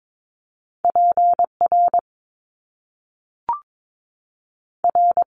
Morse code, heard as a keyed tone, about 22 wpm, 700 Hz. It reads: PR R
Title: Top Two Letter Combinations in Words 22wpm